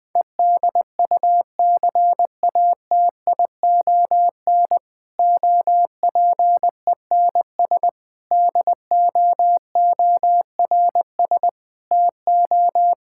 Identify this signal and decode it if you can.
Morse code, heard as a keyed tone, about 21 wpm, 700 Hz. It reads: EDUCATION OPENS DOORS TO
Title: Difficult Sentences with Hints - 20wpm